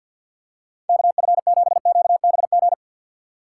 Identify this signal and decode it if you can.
Morse code, heard as a keyed tone, about 53 wpm, 700 Hz. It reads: KF7QBG
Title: USA Call Signs - 6 Characters - 6x Character Spacing - 50wpm